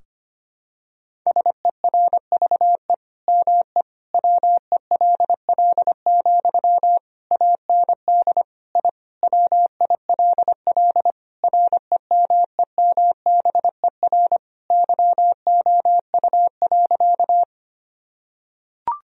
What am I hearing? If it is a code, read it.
Morse code, heard as a keyed tone, about 25 wpm, 700 Hz. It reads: SERVE ME WELL, AND I WILL REMEMBER YOU.
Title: Sentences from Top 1000 Words - Review Part 2 - 25wpm